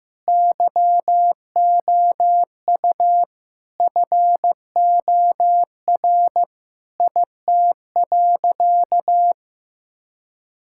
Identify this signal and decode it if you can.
Morse code, heard as a keyed tone, about 15 wpm, 700 Hz. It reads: YOU FOR IT.
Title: Sentences from Top 1000 Words - Review Part 6 - 15wpm